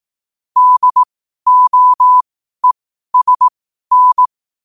Morse code, heard as a keyed tone, about 18 wpm, spaced as for 12 wpm, 1000 Hz. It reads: DOESN
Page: Quotes for Sun, 27 Apr 2025 in Morse Code at 12 words per minute.